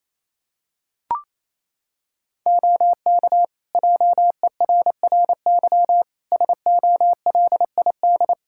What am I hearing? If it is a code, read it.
Morse code, heard as a keyed tone, about 28 wpm, 700 Hz. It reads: OK JERRY SOLID
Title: Practice QSO - Exchange 3 - More Info - 28wpm